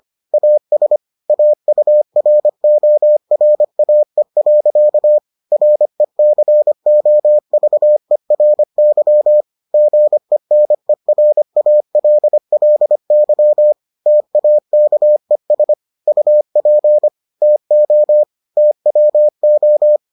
Morse code, heard as a keyed tone, about 25 wpm, 600 Hz. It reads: AS AURORAE. RECOVERY GENERALLY TAKES UP TO TWO